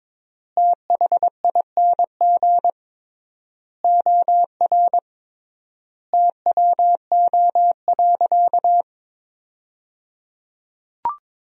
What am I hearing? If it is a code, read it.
Morse code, heard as a keyed tone, about 22 wpm, 700 Hz. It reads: THING OR TWO.
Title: 3x Word-Spacing - Sentences from Top 300 Words - 22wpm